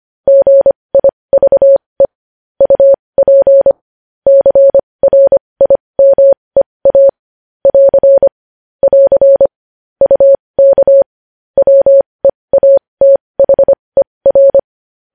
Morse code, heard as a keyed tone, about 25 wpm, spaced as for 20 wpm, 550 Hz. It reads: GIVE UP CRIMEA <AR> <AR> UK WEATHER
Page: News Headlines in Morse Code at 20 WPM